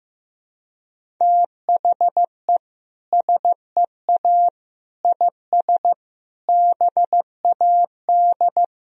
Morse code, heard as a keyed tone, about 15 wpm, 700 Hz. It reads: THE SEA IS BAD
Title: Sentences from Top 1000 Words - Review Part 1 - 15wpm